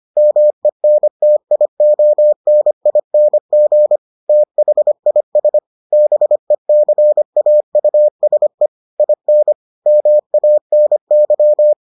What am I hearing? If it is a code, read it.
Morse code, heard as a keyed tone, about 25 wpm, 600 Hz. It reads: MENTIONING THIS BECAUSE IN MANY